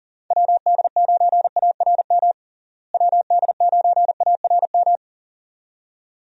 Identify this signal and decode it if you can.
Morse code, heard as a keyed tone, about 40 wpm, 700 Hz. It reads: WD9ARM WD9ARM
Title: USA Call Signs - 6 Characters - Repeated 2x - 40wpm